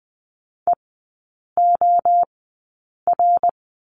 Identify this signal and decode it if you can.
Morse code, heard as a keyed tone, about 20 wpm, spaced as for 8 wpm, 700 Hz. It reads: EOR
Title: 053 Lesson 10e - Letter r - Three Characters 20wpm